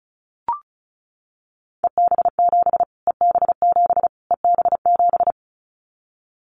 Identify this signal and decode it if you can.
Morse code, heard as a keyed tone, about 34 wpm, 700 Hz. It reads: EB7 EB7 EB7
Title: Callsign Prefixes for Top 25 DXCC - Repeated 3x - 35wpm